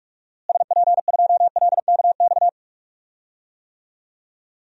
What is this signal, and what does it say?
Morse code, heard as a keyed tone, about 46 wpm, 700 Hz. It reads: SP2LKX
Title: DXCC-100 Call Signs - 6 Characters - Repeated 3x - 45wpm